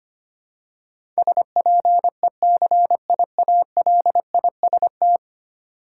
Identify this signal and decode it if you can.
Morse code, heard as a keyed tone, about 25 wpm, 700 Hz. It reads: SPECIALIST
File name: Ten Letter Words - 25wpm